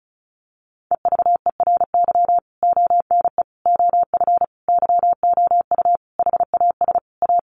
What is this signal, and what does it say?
Morse code, heard as a keyed tone, about 34 wpm, 700 Hz. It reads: EVERY ONE OF YOU HAS A
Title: Sentences from Top 1500 Words 35wpm